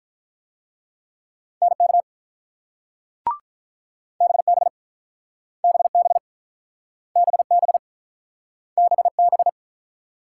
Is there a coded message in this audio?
Yes: Morse code from 1.6 to 9.5 s, heard as a keyed tone, about 48 wpm, 700 Hz. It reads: NK BB BB BB BB